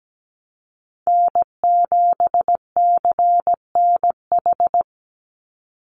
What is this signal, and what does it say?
Morse code, heard as a keyed tone, about 17 wpm, 700 Hz. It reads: N7CNH
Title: NA SOTA Chaser Callsigns 17wpm